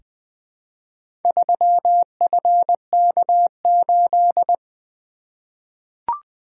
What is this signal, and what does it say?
Morse code, heard as a keyed tone, about 20 wpm, 700 Hz. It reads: 3FK8
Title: Four Letter-Number Combinations - 20wpm